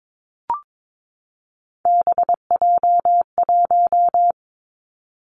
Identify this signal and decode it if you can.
Morse code, heard as a keyed tone, about 22 wpm, 700 Hz. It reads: BJ1
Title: Three Challenging Letter-Number Combinations 22wpm